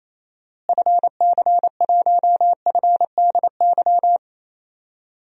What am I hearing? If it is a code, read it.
Morse code, heard as a keyed tone, about 28 wpm, 700 Hz. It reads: FC1FDY
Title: DXCC-100 Call Signs - 6 Characters - Repeated 3x - 28wpm